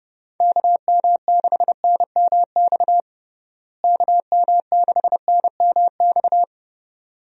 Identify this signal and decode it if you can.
Morse code, heard as a keyed tone, about 30 wpm, 700 Hz. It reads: KM6NMX KM6NMX
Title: USA Call Signs - 6 Characters - Repeated 2x - 30wpm